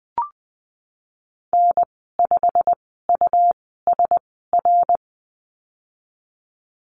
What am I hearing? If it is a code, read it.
Morse code, heard as a keyed tone, about 20 wpm, spaced as for 15 wpm, 700 Hz. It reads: N5USR
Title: 140 Lesson 23g - Letter f - Call Signs 20wpm